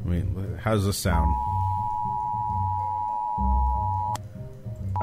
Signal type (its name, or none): Emergency Alert System (SAME) tones